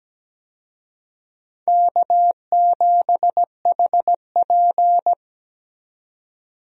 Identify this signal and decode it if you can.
Morse code, heard as a keyed tone, about 17 wpm, 700 Hz. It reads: K7HP
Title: USA Call Signs - 4 Characters - 17wpm